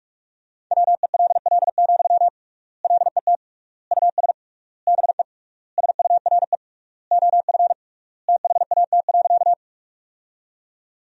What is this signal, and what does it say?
Morse code, heard as a keyed tone, about 46 wpm, 700 Hz. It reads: WELL, LET US BE SURE OF THAT.
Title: 3x Word-Spacing - Sentences from Top 300 Words - 45wpm